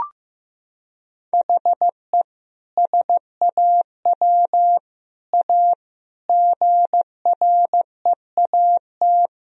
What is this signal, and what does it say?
Morse code, heard as a keyed tone, about 15 wpm, 700 Hz. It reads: HE SAW A GREAT
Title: Sentences from Top 1000 Words - Review Part 1 - 15wpm